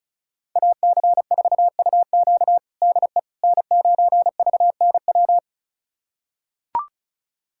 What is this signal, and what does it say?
Morse code, heard as a keyed tone, about 34 wpm, 690 Hz. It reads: AC4UQ DE N9VNW
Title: USA Callsign DE Callsign - 5 Characters - 35wpm